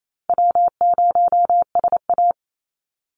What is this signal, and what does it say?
Morse code, heard as a keyed tone, about 28 wpm, 700 Hz. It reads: W0SA
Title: Call Signs Repeated 3x - 28wpm